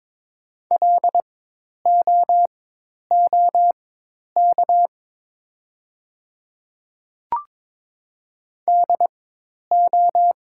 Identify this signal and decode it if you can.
Morse code, heard as a keyed tone, about 22 wpm, spaced as for 10 wpm, 690 Hz. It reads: LOOK DO
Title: 4x Character Spacing - Top 100 Words - 22wpm